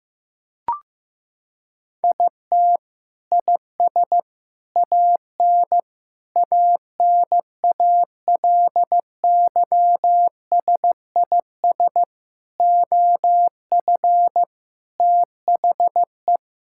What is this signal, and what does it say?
Morse code, heard as a keyed tone, about 15 wpm, 700 Hz. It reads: IT IS AN ANALYSIS OF THE
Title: Sentences from Top 1000 Words - Review Part 5 - 15wpm